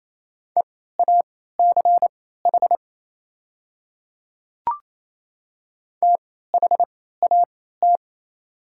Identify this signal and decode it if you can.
Morse code, heard as a keyed tone, about 28 wpm, spaced as for 16 wpm, 700 Hz. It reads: EACH THAT
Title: 3x Character Spacing - Top 100 Words - 28wpm